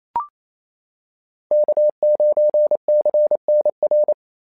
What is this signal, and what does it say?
Morse code, heard as a keyed tone, about 27 wpm, 590 Hz. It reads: K9CNR
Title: SOTA QSO Practice 28wpm